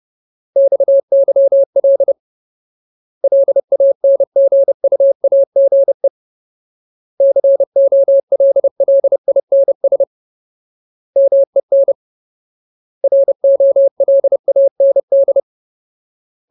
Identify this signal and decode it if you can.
Morse code, heard as a keyed tone, about 30 wpm, 550 Hz. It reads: XYL LANGUAGE COLLINS MEN ROLAND
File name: Ultimate Word List / 30WPM Ultimate